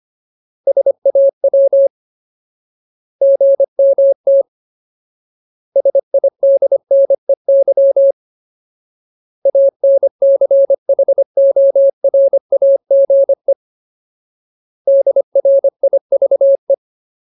Morse code, heard as a keyed tone, about 25 wpm, 550 Hz. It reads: SAW GMT SIDNEY ANCHORAGE DRIVE
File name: Ultimate Word List / 25WPM Ultimate